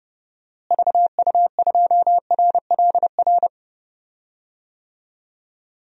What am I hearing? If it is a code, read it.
Morse code, heard as a keyed tone, about 30 wpm, 700 Hz. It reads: VU2RLR
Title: DXCC-100 Call Signs - 6 Characters - Repeated 3x - 30wpm